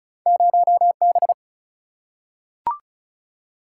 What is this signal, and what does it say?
Morse code, heard as a keyed tone, about 36 wpm, 700 Hz. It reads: 0 B
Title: Single-Letter Single-Number 35wpm